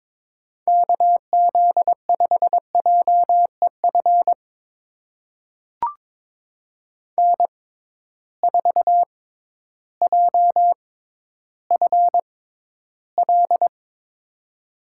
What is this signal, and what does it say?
Morse code, heard as a keyed tone, about 23 wpm, spaced as for 12 wpm, 700 Hz. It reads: KZ5JEF N4JFL
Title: USA Call Signs - 6x Character Spacing - 22wpm